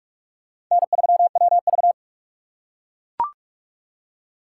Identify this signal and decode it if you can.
Morse code, heard as a keyed tone, about 40 wpm, 700 Hz. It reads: N3WV